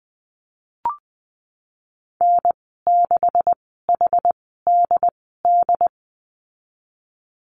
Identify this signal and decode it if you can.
Morse code, heard as a keyed tone, about 20 wpm, spaced as for 15 wpm, 700 Hz. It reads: N6HDD
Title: 128 Lesson 21g - Number 6 - Call Signs 20wpm